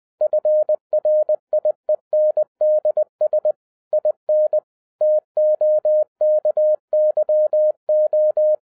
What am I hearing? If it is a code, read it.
Morse code, heard as a keyed tone, about 20 wpm, 610 Hz. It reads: FRIENDS IN TOKYO